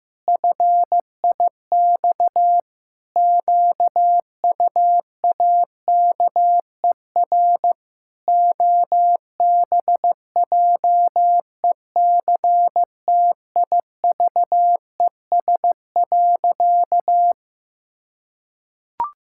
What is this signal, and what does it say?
Morse code, heard as a keyed tone, about 16 wpm, 700 Hz. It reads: FIX QUAKER OBJECTIVES.
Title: Pangram Challenge v2 - 15wpm